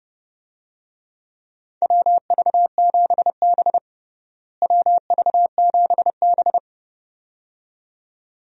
Morse code, heard as a keyed tone, about 30 wpm, 700 Hz. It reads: WV7B WV7B
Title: USA Call Signs - 4 Characters - Repeated 2x - 30wpm